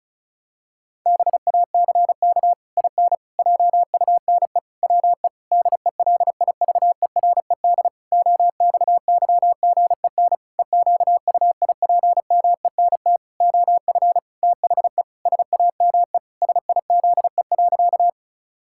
Morse code, heard as a keyed tone, about 34 wpm, 700 Hz. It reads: BACK IN JUNE WE DELIVERED OXYGEN EQUIPMENT OF THE SAME SIZE.
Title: Pangram Challenge v2 - 35wpm